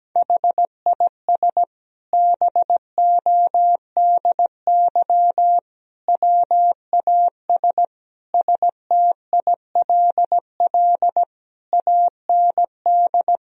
Morse code, heard as a keyed tone, about 17 wpm, 700 Hz. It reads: HIS BODY WAS STILL AND
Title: Sentences from Top 1000 Words - Review Part 2 - 17wpm